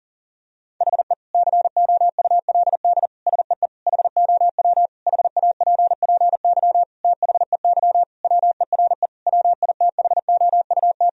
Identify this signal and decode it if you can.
Morse code, heard as a keyed tone, about 40 wpm, 700 Hz. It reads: HE COULD SEE HOW HAPPY THEY WERE WITHOUT
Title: Sentences from Top 1000 Words - Review Part 3 - 40wpm